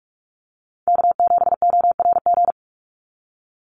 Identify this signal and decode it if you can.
Morse code, heard as a keyed tone, about 46 wpm, 700 Hz. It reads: K7ORG
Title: NA SOTA Chaser Callsigns 45wpm